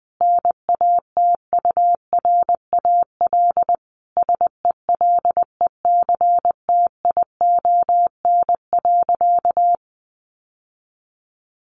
Morse code, heard as a keyed tone, about 19 wpm, 700 Hz. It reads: NATURAL SELECTION.
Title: Sentences from Top 1400 Words 20wpm